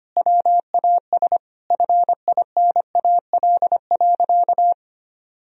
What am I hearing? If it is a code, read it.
Morse code, heard as a keyed tone, about 25 wpm, 700 Hz. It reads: WAS FINAL.
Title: Sentences from Top 1000 Words - Review Part 2 - 25wpm